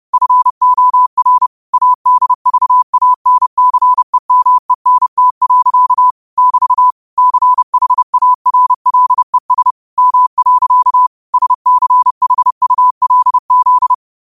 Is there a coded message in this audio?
Yes: Morse code from 0.1 to 13.9 s, heard as a keyed tone, about 30 wpm, 1000 Hz. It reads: FOR ADVANCEMENT. <BT> CHARLES M. SCHULZ